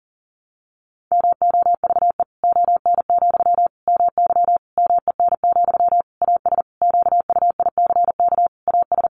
Morse code, heard as a keyed tone, about 40 wpm, 700 Hz. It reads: MOVE ON, MY MEN, AS QUICK AS
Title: Sentences from Top 500 Words 40wpm